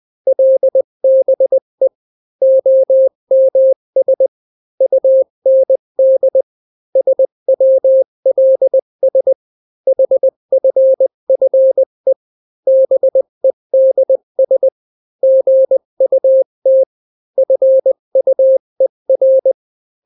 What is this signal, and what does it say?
Morse code, heard as a keyed tone, about 20 wpm, 530 Hz. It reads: LBE OMS UND SWLS HFFE BEDS GUT FUER
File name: daily / 18wpm